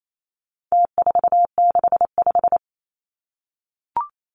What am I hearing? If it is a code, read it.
Morse code, heard as a keyed tone, about 28 wpm, 700 Hz. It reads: T465
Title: Numbers - Contest Style - 4-Digit 28wpm